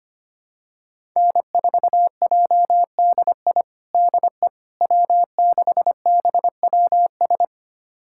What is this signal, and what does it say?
Morse code, heard as a keyed tone, about 25 wpm, 700 Hz. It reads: N4JDI DE W6BWS
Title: USA Callsign DE Callsign - 5 Characters - 25wpm